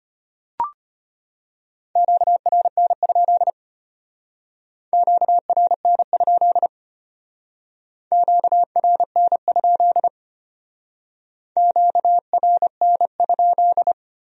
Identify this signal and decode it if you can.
Morse code, heard as a keyed tone, about 30 wpm, 700 Hz. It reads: QRN? QRN? QRN? QRN?
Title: Speed-Racing Q-Codes - 25wpm